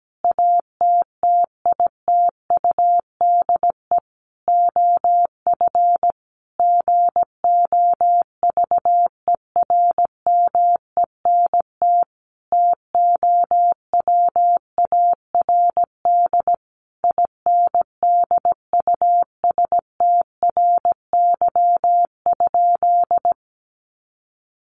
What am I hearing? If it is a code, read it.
Morse code, heard as a keyed tone, about 17 wpm, 700 Hz. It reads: ATTITUDE OF GOVERMENT TOWARD INDUSTRY?
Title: Sentences from Top 1300 Words 17wpm